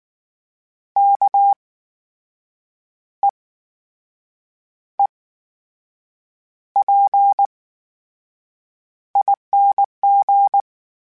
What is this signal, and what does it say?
Morse code, heard as a keyed tone, about 19 wpm, spaced as for 6 wpm, 800 Hz. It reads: KEEPING
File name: Special INGs / Special INGs-mp3s / VerySlowToFaster Ing @18